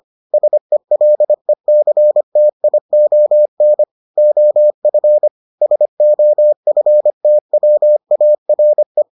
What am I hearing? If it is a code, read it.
Morse code, heard as a keyed tone, about 25 wpm, 600 Hz. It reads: SELECTION OF SOFTWARE